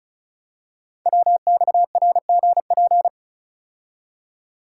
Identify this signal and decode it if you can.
Morse code, heard as a keyed tone, about 36 wpm, 700 Hz. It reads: WXRGP
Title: Code Groups - 5 Letters - 35wpm